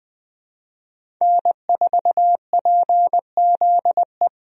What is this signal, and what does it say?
Morse code, heard as a keyed tone, about 21 wpm, 700 Hz. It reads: N4PZE
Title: USA Call Signs - 5 Characters - 20wpm